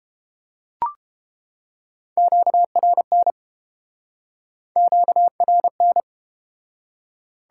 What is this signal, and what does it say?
Morse code, heard as a keyed tone, about 30 wpm, 700 Hz. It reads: QRN QRN